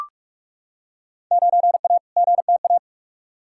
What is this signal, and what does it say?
Morse code, heard as a keyed tone, about 41 wpm, 700 Hz. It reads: 9A GTA
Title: Field Day Exchanges Encore - 45wpm